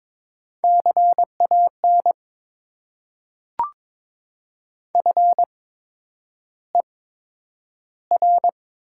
Morse code, heard as a keyed tone, about 22 wpm, spaced as for 9 wpm, 700 Hz. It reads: CAN FER
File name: Top 90 QSO Elements - 8x Character Spacing - 22wpm